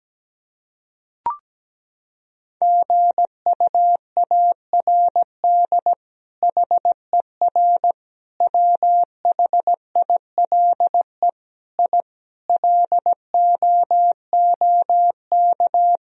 Morse code, heard as a keyed tone, about 17 wpm, 700 Hz. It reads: GUARD HER WHILE I LOOK
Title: Sentences from Top 1000 Words - Review Part 4 - 17wpm